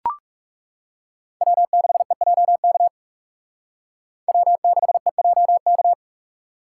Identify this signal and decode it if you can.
Morse code, heard as a keyed tone, about 36 wpm, 700 Hz. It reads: W6EJK W6EJK
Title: Speed Racing - USA 1x3 Call Signs 30wpm